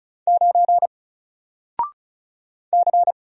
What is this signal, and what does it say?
Morse code, heard as a keyed tone, about 36 wpm, 700 Hz. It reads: 9 C
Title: Single Character (Letter:Number:Punctuation:Prosign) Alt 35wpm